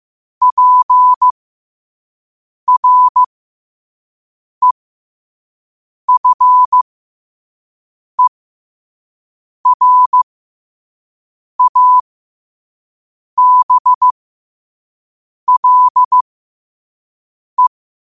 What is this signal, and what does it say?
Morse code, heard as a keyed tone, about 15 wpm, spaced as for 5 wpm, 1000 Hz. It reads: PREFERABLE
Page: New quotes every day in morse code at 5 Words per minute.